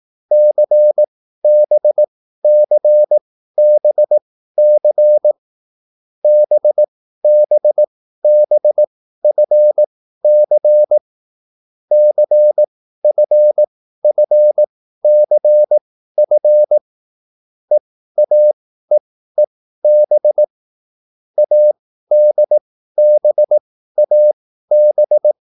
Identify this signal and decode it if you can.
Morse code, heard as a keyed tone, about 18 wpm, spaced as for 13 wpm, 600 Hz. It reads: CBCBC BBBFC CFFCF EAEEB ADBAB
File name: QSL / CW lesson